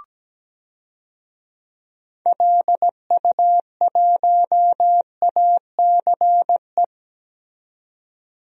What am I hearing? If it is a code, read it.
Morse code, heard as a keyed tone, about 17 wpm, 700 Hz. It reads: LU1ACE
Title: DXCC-100 Call Signs - 6 Characters - 17wpm